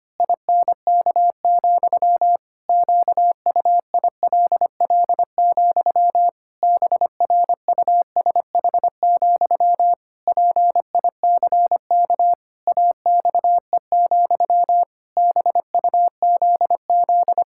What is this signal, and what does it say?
Morse code, heard as a keyed tone, about 25 wpm, 700 Hz. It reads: INK, QUILL, BRUSH, PICK AXE, BUZZ
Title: Pangram Challenge v2 - 25wpm